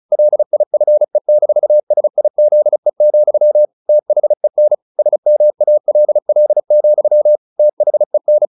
Morse code, heard as a keyed tone, about 34 wpm, 600 Hz. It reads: LIFE-SIZE, THEN SMALL, THEN
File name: the old man and the sea 2 35wpm